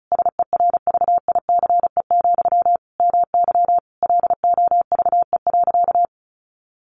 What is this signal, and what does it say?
Morse code, heard as a keyed tone, about 34 wpm, 700 Hz. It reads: SERVICE, MY LOVE.
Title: Sentences from Top 1100 Words 35wpm